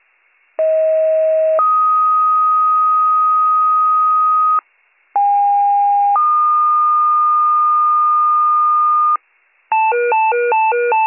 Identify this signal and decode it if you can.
Two-tone page, heard as a one-second tone, then a three-second tone, about 640 then 1250 Hz, about 810 then 1220 Hz. It is heard from 0.6 to 9.2 s.